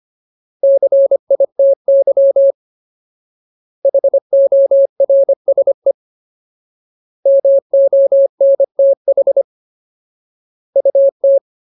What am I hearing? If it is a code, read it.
Morse code, heard as a keyed tone, about 25 wpm, 550 Hz. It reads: CITY HORSE MONTH UT
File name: Ultimate Word List / 25WPM Ultimate